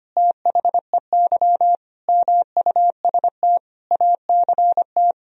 Morse code, heard as a keyed tone, about 25 wpm, 700 Hz. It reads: THEY MUST ACT
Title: Sentences from Top 1000 Words - Review Part 4 - 25wpm